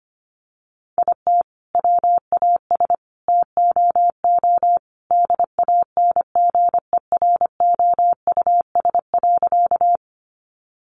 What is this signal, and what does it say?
Morse code, heard as a keyed tone, about 25 wpm, 700 Hz. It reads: IT WAS TOO DANGEROUS.
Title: Sentences from Top 1200 Words 25wpm